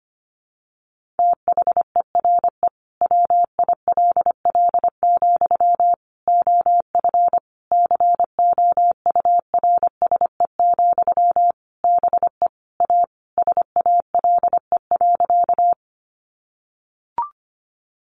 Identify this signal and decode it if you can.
Morse code, heard as a keyed tone, about 25 wpm, 700 Hz. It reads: THERE WILL, OF COURSE, BE A SALE.
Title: Sentences from Top 1400 Words 25wpm